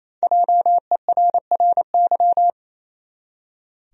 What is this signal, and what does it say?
Morse code, heard as a keyed tone, about 28 wpm, 700 Hz. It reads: JERRY
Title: USA Common Names - Max 5 Letters - 28wpm